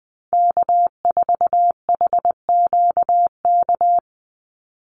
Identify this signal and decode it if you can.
Morse code, heard as a keyed tone, about 20 wpm, 700 Hz. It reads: K4HQK
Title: Call Signs - 20wpm